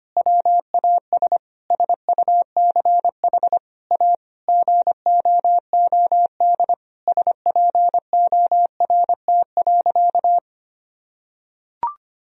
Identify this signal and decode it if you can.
Morse code, heard as a keyed tone, about 25 wpm, 700 Hz. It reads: WAS SUCH A GOOD SPORT.
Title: Sentences from Top 1000 Words - Review Part 4 - 25wpm